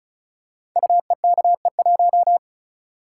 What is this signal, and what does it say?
Morse code, heard as a keyed tone, about 36 wpm, 700 Hz. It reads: UEKE1
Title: Code Groups - 5 Letter:Numbers - 35wpm